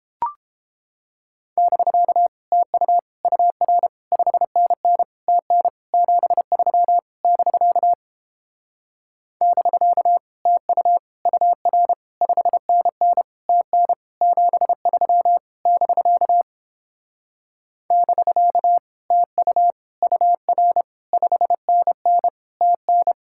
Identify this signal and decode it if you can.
Morse code, heard as a keyed tone, about 29 wpm, 700 Hz. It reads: <BK> TU UR 5NN TN 73 <BK> <BK> TU UR 5NN TN 73 <BK> <BK> TU UR 5NN TN